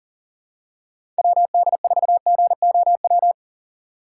Morse code, heard as a keyed tone, about 40 wpm, 700 Hz. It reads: WD4GOW
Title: USA Call Signs - 6 Characters - 4x Character Spacing - 40wpm